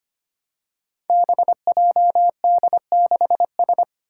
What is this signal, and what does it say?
Morse code, heard as a keyed tone, about 25 wpm, 700 Hz. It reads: BJD6S
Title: Five Challenging Letter-Number Combinations 25wpm